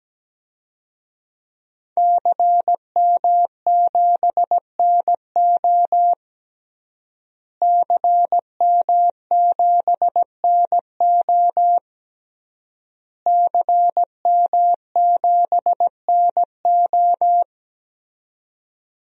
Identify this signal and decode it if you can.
Morse code, heard as a keyed tone, about 17 wpm, 700 Hz. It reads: CM7NO CM7NO CM7NO
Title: DXCC-100 Call Signs - 5 Characters - Repeated 3x - 17wpm